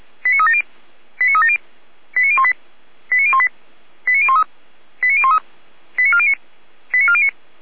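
Five-tone selective call (ZVEI), heard as quick runs of five tones, tones of about 70 ms. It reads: 89370 89370 89018 89018 89013 89013 89409 89409
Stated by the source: ZVEI five-tone selective call 89500 received by SDR, from the Feuerwehr Orlamünde (fire brigade) audio server.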